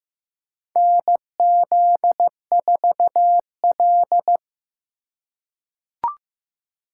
Morse code, heard as a keyed tone, about 15 wpm, 700 Hz. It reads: NZ4L
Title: USA Call Signs - 4 Characters - 2x Character Spacing - 15wpm